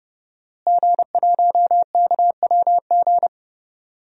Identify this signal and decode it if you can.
Morse code, heard as a keyed tone, about 30 wpm, 700 Hz. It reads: G1KWG
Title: DXCC-100 Call Signs - 5 Characters - Repeated 2x - 30wpm